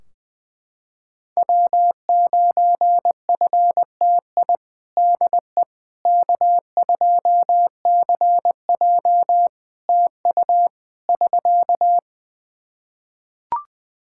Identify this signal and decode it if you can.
Morse code, heard as a keyed tone, about 20 wpm, 700 Hz. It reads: W9FTI DE K2CJ TU <SK>
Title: K1USN SST QSO Practice 20wpm